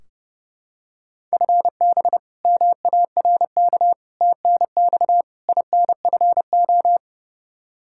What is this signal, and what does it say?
Morse code, heard as a keyed tone, about 30 wpm, 700 Hz. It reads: FB MARK TNX INFO
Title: K1USN SST QSO Practice 30wpm